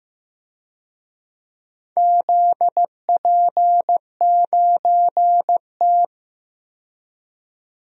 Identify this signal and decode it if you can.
Morse code, heard as a keyed tone, about 15 wpm, 700 Hz. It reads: ZP9T
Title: DXCC-100 Call Signs - 4 Characters - 15wpm